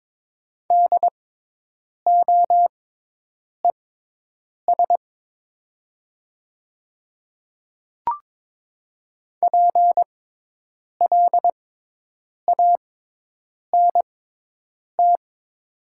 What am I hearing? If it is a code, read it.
Morse code, heard as a keyed tone, about 22 wpm, spaced as for 8 wpm, 700 Hz. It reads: DOES PLANT
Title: 6x Character Spacing - Top 300 Words - 22wpm